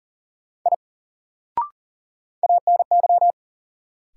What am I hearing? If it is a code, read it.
Morse code, heard as a keyed tone, about 40 wpm, 700 Hz. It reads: I ANY
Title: Sentences from Top 200 Words - New Word Drill - 40wpm